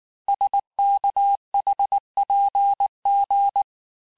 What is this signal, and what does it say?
Morse code, heard as a keyed tone, about 19 wpm, 800 Hz. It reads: SKHPG
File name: cours-du-lundi files / cours-1589-texte-chiffre-vitesse-18mpm